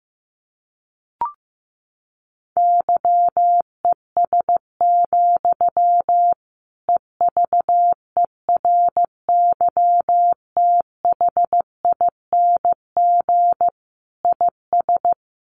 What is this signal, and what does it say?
Morse code, heard as a keyed tone, about 15 wpm, 700 Hz. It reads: YES, EVERYTHING IS
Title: Sentences from Top 1100 Words 15wpm